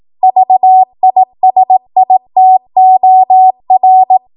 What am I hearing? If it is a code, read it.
Morse code, heard as a keyed tone, about 18 wpm, 750 Hz. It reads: VISITOR
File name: Archive / 18 WPM / 18WPM